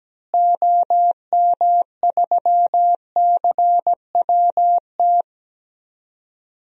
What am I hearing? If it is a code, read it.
Morse code, heard as a keyed tone, about 17 wpm, 700 Hz. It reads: OM3CWT
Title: DXCC-100 Call Signs - 6 Characters - Repeated 2x - 17wpm